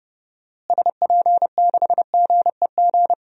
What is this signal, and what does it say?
Morse code, heard as a keyed tone, about 30 wpm, 700 Hz. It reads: SP6GEG
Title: DXCC-100 Call Signs - 6 Characters - Repeated 3x - 30wpm